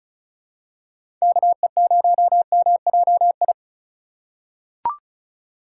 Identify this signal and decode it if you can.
Morse code, heard as a keyed tone, about 34 wpm, 700 Hz. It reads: KE0MJI
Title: USA Call Signs - 6 Characters - 4x Character Spacing - 35wpm